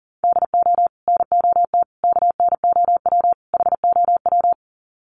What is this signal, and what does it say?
Morse code, heard as a keyed tone, about 40 wpm, 700 Hz. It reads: DO NOT KNOW HOW
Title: Sets of 4 Words 40wpm